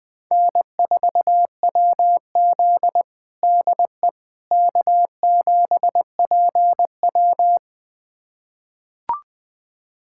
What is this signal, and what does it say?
Morse code, heard as a keyed tone, about 20 wpm, 700 Hz. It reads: N4WZ DE K7PW
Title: USA Callsign DE Callsign - 4 Characters - 20wpm